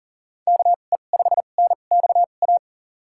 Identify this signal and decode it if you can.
Morse code, heard as a keyed tone, about 40 wpm, spaced as for 29 wpm, 700 Hz. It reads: KE5NXA
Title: USA Call Signs - 6 Characters - 2x Character Spacing - 40wpm